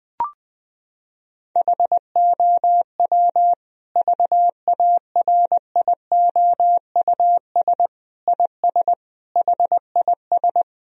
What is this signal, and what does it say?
Morse code, heard as a keyed tone, about 20 wpm, 700 Hz. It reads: HOW VARIOUS IS HIS
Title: Sentences from Top 1000 Words - Review Part 4 - 20wpm